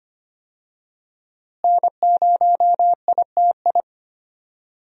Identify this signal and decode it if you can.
Morse code, heard as a keyed tone, about 25 wpm, 700 Hz. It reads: N0ITI
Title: USA Call Signs - 5 Characters - Repeated 2x - 25wpm